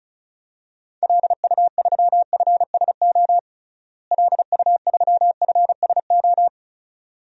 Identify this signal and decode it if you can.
Morse code, heard as a keyed tone, about 34 wpm, 700 Hz. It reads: LU3FSO LU3FSO
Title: DXCC-100 Call Signs - 6 Characters - Repeated 2x - 35wpm